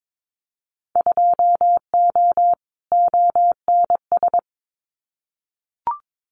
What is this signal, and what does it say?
Morse code, heard as a keyed tone, about 22 wpm, 700 Hz. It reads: 2O ONS
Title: Winter Field Day Exchanges Encore 22wpm